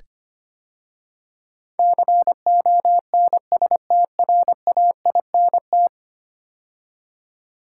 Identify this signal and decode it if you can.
Morse code, heard as a keyed tone, about 25 wpm, 700 Hz. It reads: CONSTRAINT
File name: 10 Letter Words - First 3 Letters Given - 25wpm